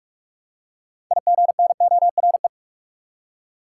Morse code, heard as a keyed tone, about 43 wpm, 700 Hz. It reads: IGNORE
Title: Six Letter Words - 45wpm